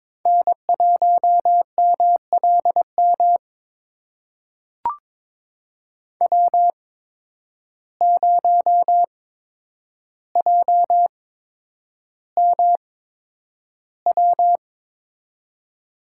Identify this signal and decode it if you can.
Morse code, heard as a keyed tone, about 22 wpm, spaced as for 9 wpm, 700 Hz. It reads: N1MLM W0JMW
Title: USA Call Signs - 5 Characters - 8x Character Spacing - 22wpm